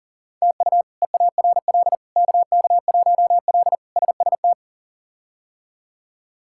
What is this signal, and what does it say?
Morse code, heard as a keyed tone, about 40 wpm, 700 Hz. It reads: TU EARL KK1L SST
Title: K1USN SST QSO Practice - 40wpm